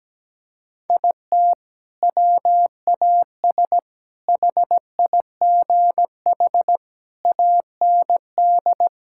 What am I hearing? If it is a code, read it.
Morse code, heard as a keyed tone, about 17 wpm, 700 Hz. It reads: IT WAS HIGH AND
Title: Sentences from Top 1600 Words 17wpm